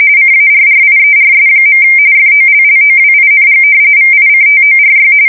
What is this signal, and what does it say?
RTTY, about 45.45 baud. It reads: CARRIERS. FOLLOWING VICTORY
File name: fsk170 / usnavy4